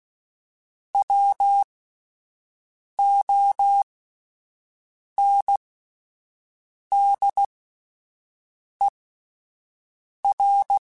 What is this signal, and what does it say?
Morse code, heard as a keyed tone, about 16 wpm, spaced as for 5 wpm, 800 Hz. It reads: WONDER
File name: Alphabetical / Alphabetical mp3s / Ws @5